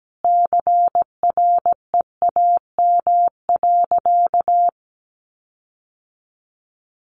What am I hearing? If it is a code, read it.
Morse code, heard as a keyed tone, about 17 wpm, 700 Hz. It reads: CREAM.
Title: Sentences from Top 1500 Words 17wpm